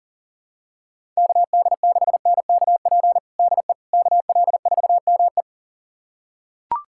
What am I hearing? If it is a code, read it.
Morse code, heard as a keyed tone, about 40 wpm, 700 Hz. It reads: KD6NKP DE KL4ME
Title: USA Callsign DE Callsign 40wpm